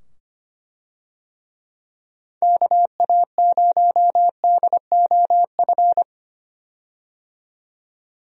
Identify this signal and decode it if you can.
Morse code, heard as a keyed tone, about 25 wpm, 700 Hz. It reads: KA0DOF
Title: USA Call Signs - 6 Characters - 25wpm